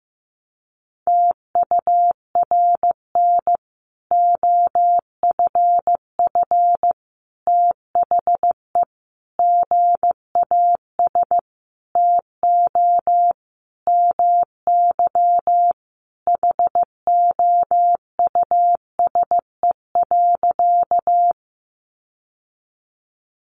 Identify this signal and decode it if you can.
Morse code, heard as a keyed tone, about 15 wpm, 700 Hz. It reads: TURN OFF THE GAS TO MY HOUSE.
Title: Sentences from Top 1100 Words 15wpm